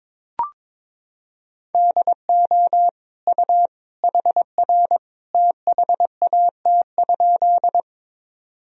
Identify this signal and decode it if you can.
Morse code, heard as a keyed tone, about 22 wpm, 700 Hz. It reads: DO U HR THAT?
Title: Sentences from Top 300 Words and Abbreviations 22wpm